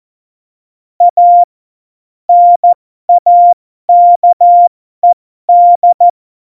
Morse code, heard as a keyed tone, about 14 wpm, spaced as for 12 wpm, 700 Hz. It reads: A NAKED
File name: Morsekurs / Witze